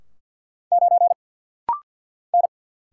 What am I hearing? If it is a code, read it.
Morse code, heard as a keyed tone, about 50 wpm, 700 Hz. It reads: 9 N
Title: Single Letter-Number Alt - Rapid-Fire 50wpm